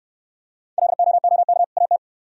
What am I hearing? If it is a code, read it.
Morse code, heard as a keyed tone, about 66 wpm, 700 Hz. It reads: LOOK AT